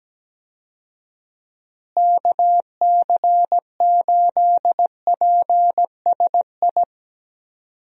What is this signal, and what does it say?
Morse code, heard as a keyed tone, about 17 wpm, 700 Hz. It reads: KC8PSI
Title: USA Call Signs - 6 Characters - Repeated 3x - 17wpm